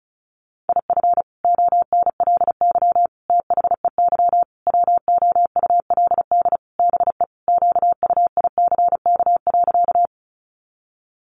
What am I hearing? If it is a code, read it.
Morse code, heard as a keyed tone, about 34 wpm, 700 Hz. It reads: IF ONLY THEY WOULD BE QUICK.
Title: Sentences from Top 500 Words 35wpm